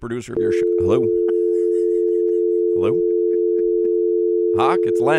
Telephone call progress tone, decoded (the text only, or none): dial tone